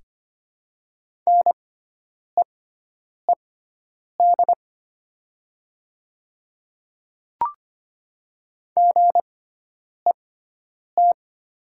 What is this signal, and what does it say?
Morse code, heard as a keyed tone, about 25 wpm, spaced as for 9 wpm, 700 Hz. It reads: NEED GET
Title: 6x Character Spacing - Top 200 Words - 25wpm